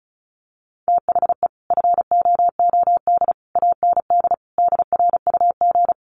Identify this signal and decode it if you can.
Morse code, heard as a keyed tone, about 34 wpm, 700 Hz. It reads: THE FOOD AND DRUG
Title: Sets of 5 Words 35wpm